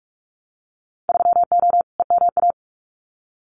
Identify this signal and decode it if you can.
Morse code, heard as a keyed tone, about 46 wpm, 700 Hz. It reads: 3O EMA
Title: Winter Field Day Exchanges Encore 45wpm